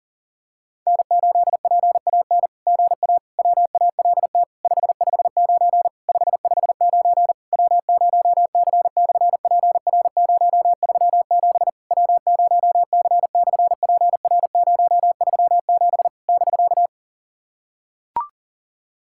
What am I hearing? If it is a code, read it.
Morse code, heard as a keyed tone, about 40 wpm, 700 Hz. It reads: N8PAN GA WALT 559 559 W0C/PR037 W0C/PR037 <BK>
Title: SOTA QSO Practice 40wpm